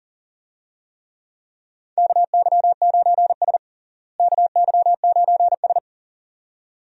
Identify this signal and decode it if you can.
Morse code, heard as a keyed tone, about 37 wpm, 700 Hz. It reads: KY9S KY9S
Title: USA Call Signs - 4 Characters - Repeated 2x - 40wpm